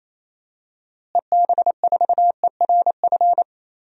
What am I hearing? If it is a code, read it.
Morse code, heard as a keyed tone, about 28 wpm, 700 Hz. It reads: EB4ERF
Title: DXCC-100 Call Signs - 6 Characters - Repeated 2x - 28wpm